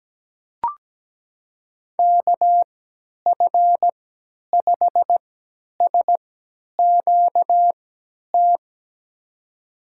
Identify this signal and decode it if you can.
Morse code, heard as a keyed tone, about 17 wpm, spaced as for 10 wpm, 700 Hz. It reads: KF5SQT
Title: USA Call Signs - 6 Characters - 3x Character Spacing - 17wpm